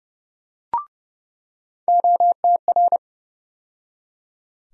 Morse code, heard as a keyed tone, about 30 wpm, 700 Hz. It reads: OTR
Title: USA Call Sign Postfixes - 3 Letters 30wpm